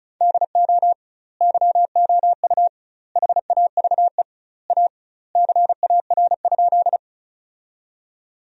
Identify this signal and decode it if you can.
Morse code, heard as a keyed tone, about 36 wpm, 700 Hz. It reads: DO YOU HAVE A CAR?
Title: 2x Word-Spacing - Sentences from Top 300 Words - 35wpm